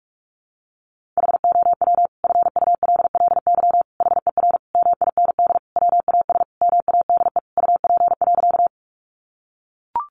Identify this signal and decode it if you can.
Morse code, heard as a keyed tone, about 44 wpm, 700 Hz. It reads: HOW FULLY HER MIND WAS MADE UP.
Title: Sentences from Top 1300 Words 45wpm